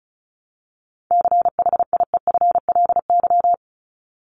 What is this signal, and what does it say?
Morse code, heard as a keyed tone, about 34 wpm, 700 Hz. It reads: CHIEFLY
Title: LY Suffix - Top 100 Words with Base Words 35wpm